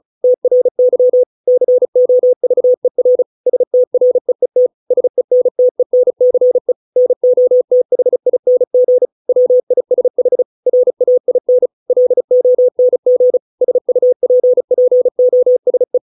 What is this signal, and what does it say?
Morse code, heard as a keyed tone, about 34 wpm, 500 Hz. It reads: TRY COVER STREET SENTENCE NOTHING WISH RAIN LONG SUPPOSE